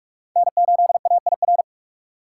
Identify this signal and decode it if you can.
Morse code, heard as a keyed tone, about 45 wpm, 700 Hz. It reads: N 8 A I R
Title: USA Call Signs - 5 Characters - 6x Character Spacing - 45wpm